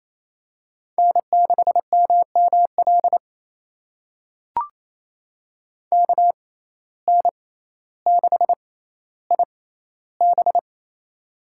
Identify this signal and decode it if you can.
Morse code, heard as a keyed tone, about 28 wpm, spaced as for 14 wpm, 700 Hz. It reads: N6MML KN6IB
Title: USA Call Signs - 5 Characters - 6x Character Spacing - 28wpm